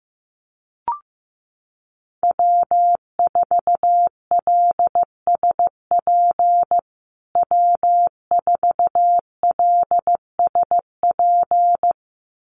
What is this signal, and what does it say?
Morse code, heard as a keyed tone, about 15 wpm, 700 Hz. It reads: W4LSP W4LSP
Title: USA Call Signs Repeated 2x 15wpm